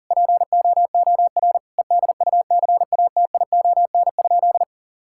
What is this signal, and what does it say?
Morse code, heard as a keyed tone, about 40 wpm, 700 Hz. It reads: POOR EDUCATION?
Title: Sentences from Top 1000 Words - Review Part 6 - 40wpm